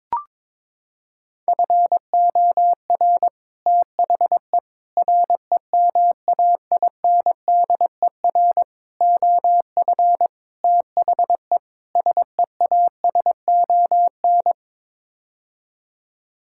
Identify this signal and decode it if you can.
Morse code, heard as a keyed tone, about 22 wpm, 700 Hz. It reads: FOR THE REMAINDER OF THE SEASON
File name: practice / 22wpm / Sets of 6 Words - 22wpm